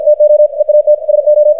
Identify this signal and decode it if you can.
Morse code, heard as a keyed tone, about 53 wpm, 600 Hz, with radio noise. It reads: TO EAT SO
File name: archived / CRSnet